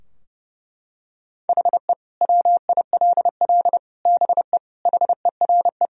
Morse code, heard as a keyed tone, about 29 wpm, 700 Hz. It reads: HE WILL BE HERE
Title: Sentences from Top 1000 Words - Review Part 3 - 30wpm